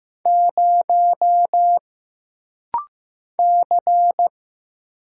Morse code, heard as a keyed tone, about 15 wpm, 700 Hz. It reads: 0 C
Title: Single Letter-Number Alt - Rapid-Fire 15wpm